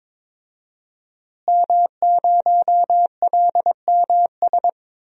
Morse code, heard as a keyed tone, about 22 wpm, 700 Hz. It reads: M0LMS
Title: DXCC-100 Call Signs - 5 Characters - Repeated 2x - 22wpm